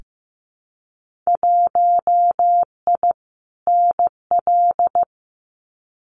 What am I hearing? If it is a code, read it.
Morse code, heard as a keyed tone, about 15 wpm, 700 Hz. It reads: 1I NL
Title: Winter Field Day Exchanges 15wpm